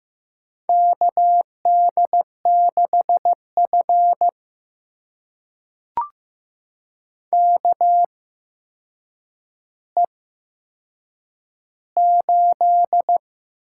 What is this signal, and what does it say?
Morse code, heard as a keyed tone, about 15 wpm, spaced as for 7 wpm, 700 Hz. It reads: KD6F KE8
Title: USA Call Signs - 8x Character Spacing - 15wpm